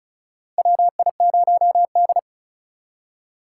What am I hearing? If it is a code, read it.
Morse code, heard as a keyed tone, about 36 wpm, 700 Hz. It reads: WI0D
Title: USA Call Signs - 4 Characters - 2x Character Spacing - 35wpm